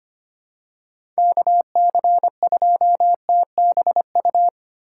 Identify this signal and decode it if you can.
Morse code, heard as a keyed tone, about 25 wpm, 690 Hz. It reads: KC2TBU
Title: USA Call Signs - 6 Characters - 3x Character Spacing - 25wpm